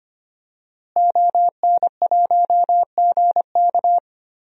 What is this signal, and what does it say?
Morse code, heard as a keyed tone, about 25 wpm, 700 Hz. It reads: ON1GK
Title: DXCC-100 Call Signs - 5 Characters - Repeated 2x - 25wpm